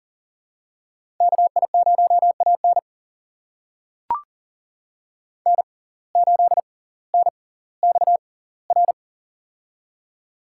Morse code, heard as a keyed tone, about 40 wpm, spaced as for 20 wpm, 700 Hz. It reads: KI0AN N8NXR
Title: USA Call Signs - 5 Characters - 6x Character Spacing - 40wpm